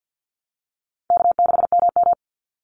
Morse code, heard as a keyed tone, about 50 wpm, 700 Hz. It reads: K6MM
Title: Call Signs Repeated 2x - 50wpm